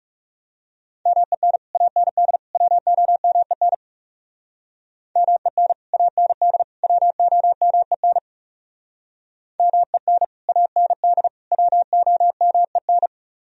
Morse code, heard as a keyed tone, about 37 wpm, 700 Hz. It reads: MEN AND WOMEN MEN AND WOMEN MEN AND WOMEN
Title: Speed-Racing - Sets of 3 Words 30wpm